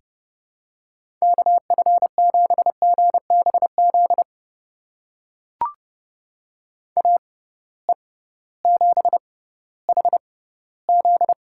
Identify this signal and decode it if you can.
Morse code, heard as a keyed tone, about 30 wpm, spaced as for 16 wpm, 700 Hz. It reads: KF7GBZ AE7HZ
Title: USA Call Signs - 6x Character Spacing - 30wpm